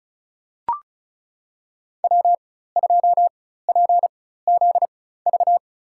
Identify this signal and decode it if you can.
Morse code, heard as a keyed tone, about 34 wpm, spaced as for 16 wpm, 700 Hz. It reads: W2PZV
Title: USA Call Signs - 5 Characters - 4x Character Spacing - 35wpm